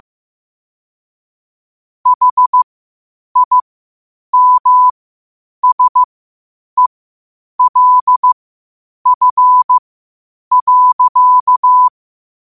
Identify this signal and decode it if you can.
Morse code, heard as a keyed tone, about 15 wpm, spaced as for 8 wpm, 1000 Hz. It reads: HIMSELF.
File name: qotdfiles / 8wpm